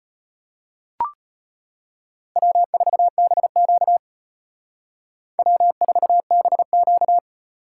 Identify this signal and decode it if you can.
Morse code, heard as a keyed tone, about 32 wpm, 700 Hz. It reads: W4BQ W4BQ